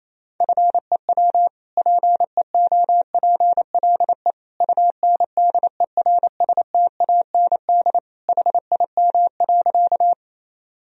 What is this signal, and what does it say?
Morse code, heard as a keyed tone, about 28 wpm, 700 Hz. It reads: FEW PEOPLE UNDERSTAND HIM.
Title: Sentences from Top 1000 Words - Review Part 1 - 28wpm